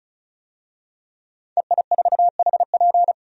Morse code, heard as a keyed tone, about 36 wpm, 700 Hz. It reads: EI4HP
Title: DXCC-100 Call Signs - 5 Characters - 35wpm